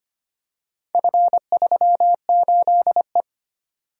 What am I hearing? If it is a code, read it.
Morse code, heard as a keyed tone, about 25 wpm, 700 Hz. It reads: F38E
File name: practice / 25wpm / Four Letter-Number Combinations - 25wpm